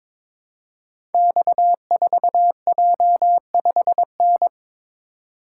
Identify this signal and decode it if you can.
Morse code, heard as a keyed tone, about 22 wpm, 700 Hz. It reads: X4J5N
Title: Code Groups - 5 Letter:Numbers - 22wpm